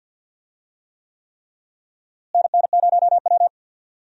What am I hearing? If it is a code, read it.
Morse code, heard as a keyed tone, about 52 wpm, 700 Hz. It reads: NN0W
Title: USA Call Signs - 4 Characters - Repeated 3x - 50wpm